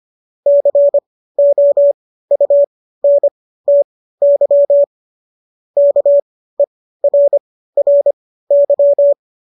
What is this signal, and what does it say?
Morse code, heard as a keyed tone, about 25 wpm, spaced as for 15 wpm, 570 Hz. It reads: COUNTY KERRY